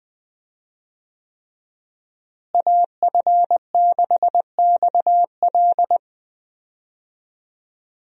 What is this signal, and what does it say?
Morse code, heard as a keyed tone, about 20 wpm, 700 Hz. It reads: AF6XL
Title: USA Call Signs - 5 Characters - Repeated 3x - 20wpm